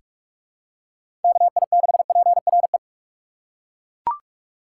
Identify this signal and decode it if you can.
Morse code, heard as a keyed tone, about 42 wpm, 700 Hz. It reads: KI6PRE
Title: USA Call Signs - 6 Characters - 3x Character Spacing - 45wpm